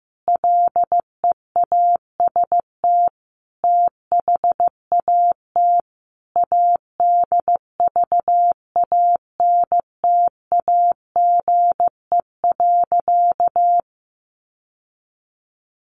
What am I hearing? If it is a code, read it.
Morse code, heard as a keyed tone, about 15 wpm, 700 Hz. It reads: LEAST THAT ADVANTAGE.
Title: Sentences from Top 900 Words 15wpm